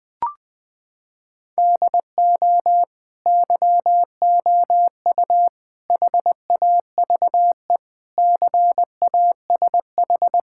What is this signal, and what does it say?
Morse code, heard as a keyed tone, about 20 wpm, 700 Hz. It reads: DO YOU HAVE CASH
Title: Sentences from Top 1000 Words - Review Part 5 - 20wpm